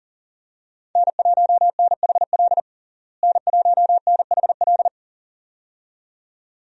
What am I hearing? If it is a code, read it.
Morse code, heard as a keyed tone, about 40 wpm, 700 Hz. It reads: N1NHL N1NHL
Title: USA Call Signs - 5 Characters - Repeated 2x - 40wpm